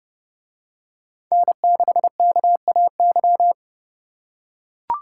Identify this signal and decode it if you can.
Morse code, heard as a keyed tone, about 29 wpm, 700 Hz. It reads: N6KAY